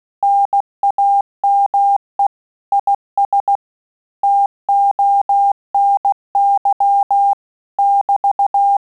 Morse code, heard as a keyed tone, about 16 wpm, 800 Hz. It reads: NAME IS TONY <BT>
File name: Telegraphy-group / QSO CW 15 PPM